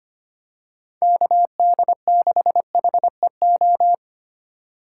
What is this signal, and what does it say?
Morse code, heard as a keyed tone, about 25 wpm, 700 Hz. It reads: KD6HEO
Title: USA Call Signs - 4x Character Spacing - 25wpm